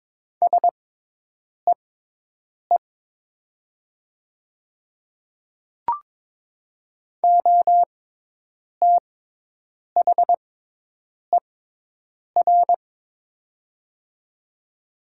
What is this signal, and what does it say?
Morse code, heard as a keyed tone, about 22 wpm, spaced as for 8 wpm, 700 Hz. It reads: SEE OTHER
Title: 6x Character Spacing - Top 200 Words - 22wpm